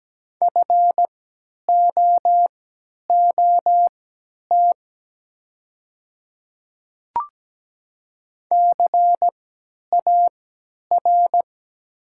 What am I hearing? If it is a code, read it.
Morse code, heard as a keyed tone, about 17 wpm, spaced as for 10 wpm, 700 Hz. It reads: FOOT CAR
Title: 3x Character Spacing - Top 500 Words - 17wpm